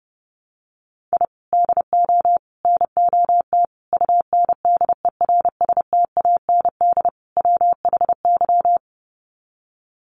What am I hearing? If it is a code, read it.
Morse code, heard as a keyed tone, about 30 wpm, 700 Hz. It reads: I DO NOT UNDERSTAND WHY
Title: Sets of 5 Words Encore 30wpm